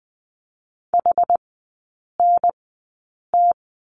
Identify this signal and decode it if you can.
Morse code, heard as a keyed tone, about 20 wpm, spaced as for 8 wpm, 700 Hz. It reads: HNT
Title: 145 Lesson 24e - Letter y - Three Characters 20wpm